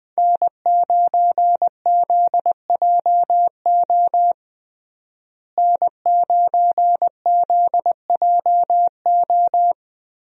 Morse code, heard as a keyed tone, about 20 wpm, 700 Hz. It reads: N9ZJO N9ZJO
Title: USA Call Signs - 5 Characters - Repeated 3x - 20wpm